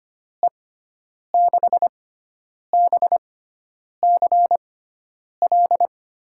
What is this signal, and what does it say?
Morse code, heard as a keyed tone, about 25 wpm, spaced as for 9 wpm, 700 Hz. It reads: E6BCL
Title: USA Call Signs - 6 Characters - 6x Character Spacing - 25wpm